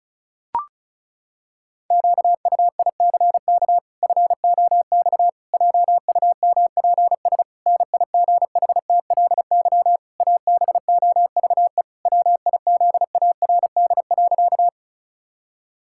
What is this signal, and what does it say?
Morse code, heard as a keyed tone, about 36 wpm, 700 Hz. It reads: QUICK FOX JUMPS NIGHTLY ABOVE WIZARD.
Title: Pangram Challenge v2 - 35wpm